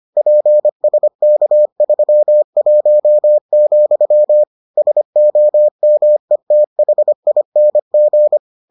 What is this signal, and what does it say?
Morse code, heard as a keyed tone, about 25 wpm, 600 Hz. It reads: PSK31, SOMETHING